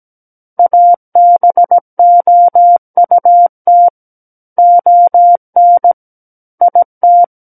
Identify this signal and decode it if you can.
Morse code, heard as a keyed tone, about 17 wpm, 700 Hz. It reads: ABOUT ON IT
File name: Book One / 08-Chapter 8 - 17 WPM